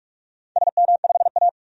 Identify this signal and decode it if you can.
Morse code, heard as a keyed tone, about 45 wpm, 700 Hz. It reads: SM5A
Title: DXCC-100 Call Signs - 4 Characters - 45wpm